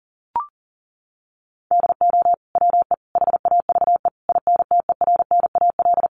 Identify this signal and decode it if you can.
Morse code, heard as a keyed tone, about 40 wpm, 700 Hz. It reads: DO WE HAVE INTERNAL
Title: Sentences from Top 1400 Words 40wpm